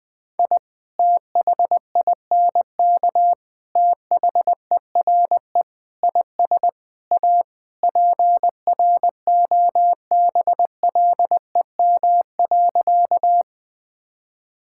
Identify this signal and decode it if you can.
Morse code, heard as a keyed tone, about 20 wpm, 700 Hz. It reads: I THINK THERE IS A PROBLEM.
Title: Sentences with 5-Word Phrases - 20wpm